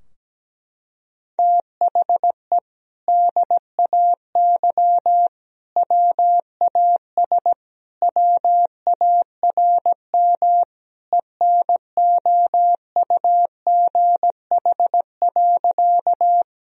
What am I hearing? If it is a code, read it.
Morse code, heard as a keyed tone, about 17 wpm, 700 Hz. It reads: THE DAY WAS WARM ENOUGH.
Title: Sentences from Top 1000 Words - Review Part 2 - 17wpm